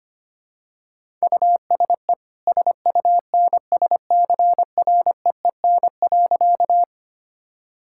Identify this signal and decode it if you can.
Morse code, heard as a keyed tone, about 25 wpm, 700 Hz. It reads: USE SUNSCREEN.
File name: Easy Sentences with Hints - 25wpm